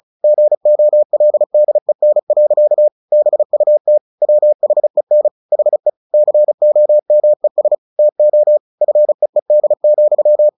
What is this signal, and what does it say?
Morse code, heard as a keyed tone, about 34 wpm, 600 Hz. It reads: GOLDEN. BUT WHEN HE COMES TO FEED,
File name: the old man and the sea 2 35wpm